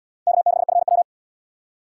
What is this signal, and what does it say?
Morse code, heard as a keyed tone, about 77 wpm, 700 Hz. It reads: K6FW